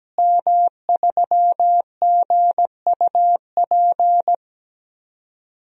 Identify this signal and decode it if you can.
Morse code, heard as a keyed tone, about 17 wpm, 700 Hz. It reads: M3GUP
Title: DXCC-100 Call Signs - 5 Characters - Repeated 2x - 17wpm